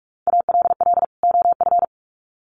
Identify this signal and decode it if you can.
Morse code, heard as a keyed tone, about 44 wpm, 700 Hz. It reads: ALL OF
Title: Sets of 2 Words 45wpm